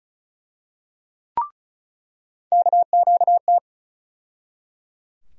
Morse code, heard as a keyed tone, about 36 wpm, 700 Hz. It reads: KQT
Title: USA Call Sign Postfixes - 3 Letters 35wpm